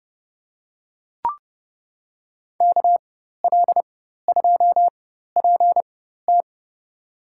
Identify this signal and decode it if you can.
Morse code, heard as a keyed tone, about 30 wpm, spaced as for 14 wpm, 700 Hz. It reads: KL2PT
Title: USA Call Signs - 4x Character Spacing - 30wpm